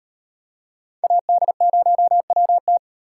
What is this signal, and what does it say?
Morse code, heard as a keyed tone, about 38 wpm, 700 Hz. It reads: AD0WT